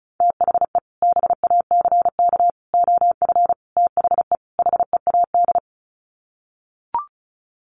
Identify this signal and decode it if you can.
Morse code, heard as a keyed tone, about 34 wpm, 700 Hz. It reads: THE BACK OF THE HEAD
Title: Sets of 5 Words Encore2 35wpm